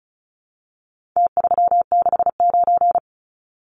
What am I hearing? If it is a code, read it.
Morse code, heard as a keyed tone, about 34 wpm, 700 Hz. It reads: T369
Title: Numbers - Contest Style - 4-Digit 35wpm